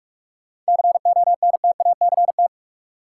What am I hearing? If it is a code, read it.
Morse code, heard as a keyed tone, about 45 wpm, 700 Hz. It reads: CONTACT
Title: Sentences from Top 1000 Words - Review Part 7 - New Word Drill - 45wpm